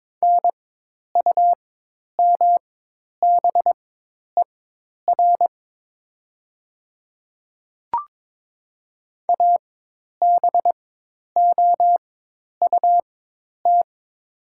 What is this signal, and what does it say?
Morse code, heard as a keyed tone, about 21 wpm, spaced as for 10 wpm, 690 Hz. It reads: NUMBER ABOUT
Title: 4x Character Spacing - Top 500 Words - 22wpm